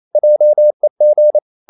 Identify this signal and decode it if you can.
Morse code, heard as a keyed tone, about 28 wpm, 600 Hz. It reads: JEG
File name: Ferie QRQ 28wpm